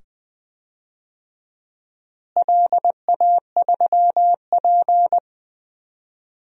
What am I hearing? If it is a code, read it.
Morse code, heard as a keyed tone, about 21 wpm, 700 Hz. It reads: LA3P
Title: DXCC-100 Call Signs - 4 Characters - Repeated 2x - 20wpm